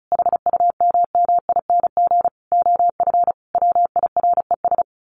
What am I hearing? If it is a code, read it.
Morse code, heard as a keyed tone, about 34 wpm, 700 Hz. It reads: HUMMING OF WIRES
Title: A Sound Of Thunder 35wpm